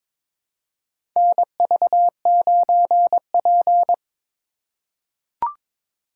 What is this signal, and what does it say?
Morse code, heard as a keyed tone, about 22 wpm, 700 Hz. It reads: NV9P
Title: USA Call Signs - 4x Character Spacing - 22wpm